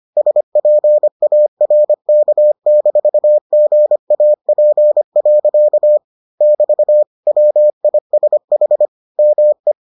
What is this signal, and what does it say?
Morse code, heard as a keyed tone, about 25 wpm, 600 Hz. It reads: SPARK-GAP. <BT> WISH ME